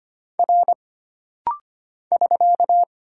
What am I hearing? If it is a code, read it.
Morse code, heard as a keyed tone, about 25 wpm, 700 Hz. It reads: R <SK>
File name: Single Character (Letter:Number:Punctuation:Prosign) - Rapid-Fire - 25wpm